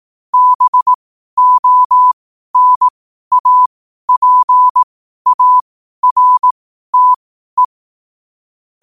Morse code, heard as a keyed tone, about 18 wpm, spaced as for 12 wpm, 1000 Hz. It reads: BONAPARTE
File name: qotdfiles / 12wpm